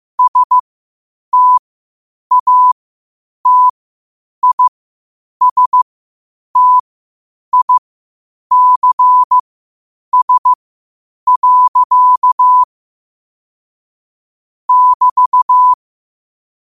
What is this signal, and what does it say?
Morse code, heard as a keyed tone, about 15 wpm, spaced as for 8 wpm, 1000 Hz. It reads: STATISTICS. <BT>
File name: qotdfiles / 8wpm